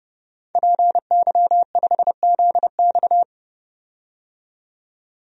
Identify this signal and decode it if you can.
Morse code, heard as a keyed tone, about 30 wpm, 700 Hz. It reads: PY5ZX
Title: DXCC-100 Call Signs - 5 Characters - 30wpm